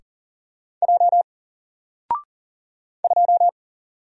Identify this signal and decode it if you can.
Morse code, heard as a keyed tone, about 40 wpm, 700 Hz. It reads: J 2
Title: Single Letter-Number 40wpm